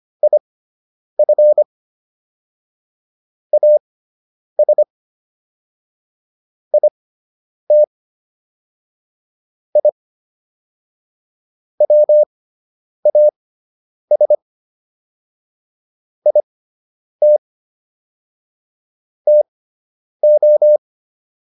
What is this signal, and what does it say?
Morse code, heard as a keyed tone, about 25 wpm, spaced as for 9 wpm, 600 Hz. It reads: IF AS IT I WAS IT TO